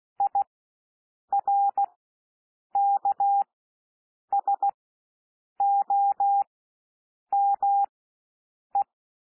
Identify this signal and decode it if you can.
Morse code, heard as a keyed tone, about 16 wpm, spaced as for 7 wpm, 800 Hz. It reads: IRKSOME
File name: cwmp3 / W1 SOUND WORDS 16 wpm